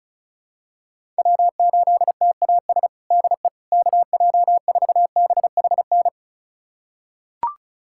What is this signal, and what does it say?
Morse code, heard as a keyed tone, about 34 wpm, 700 Hz. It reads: W8TAS DE KJ4BHN
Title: USA Callsign DE Callsign 35wpm